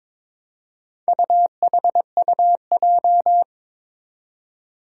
Morse code, heard as a keyed tone, about 22 wpm, 700 Hz. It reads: UHUJ
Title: Four Challenging Letter-Number Combinations 22wpm